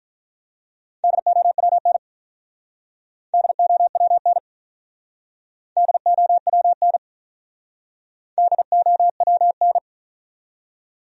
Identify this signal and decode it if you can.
Morse code, heard as a keyed tone, about 47 wpm, 700 Hz. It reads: DOWN DOWN DOWN DOWN